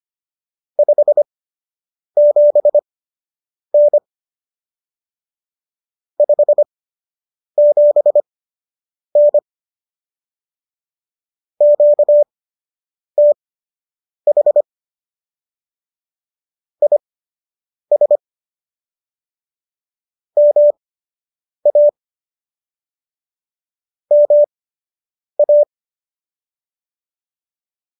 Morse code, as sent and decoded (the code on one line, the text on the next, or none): ..... --... -. / ..... --... -. / --.- - .... / .. ... / -- .- / -- .-
57N 57N QTH IS MA MA